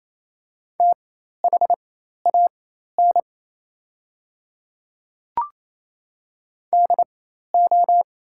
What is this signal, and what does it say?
Morse code, heard as a keyed tone, about 28 wpm, spaced as for 13 wpm, 700 Hz. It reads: THAN DO
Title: 4x Character Spacing - Top 100 Words - 28wpm